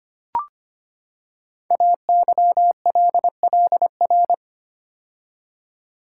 Morse code, heard as a keyed tone, about 25 wpm, 700 Hz. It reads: AYLLR
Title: Code Groups - 5 Letters - 25wpm